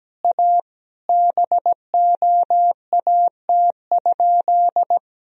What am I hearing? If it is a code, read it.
Morse code, heard as a keyed tone, about 17 wpm, 700 Hz. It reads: A BOAT?
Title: Sentences from Top 1000 Words - Review Part 3 - 17wpm